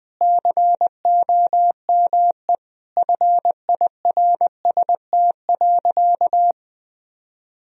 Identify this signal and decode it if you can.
Morse code, heard as a keyed tone, about 20 wpm, 700 Hz. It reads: COME FIRST.
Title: Sentences from Top 1000 Words - Review Part 1 - 20wpm